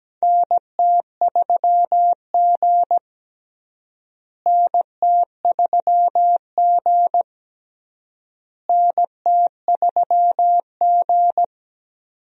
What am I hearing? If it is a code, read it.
Morse code, heard as a keyed tone, about 17 wpm, 700 Hz. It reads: NT3G NT3G NT3G
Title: USA Call Signs - 4 Characters - Repeated 3x - 17wpm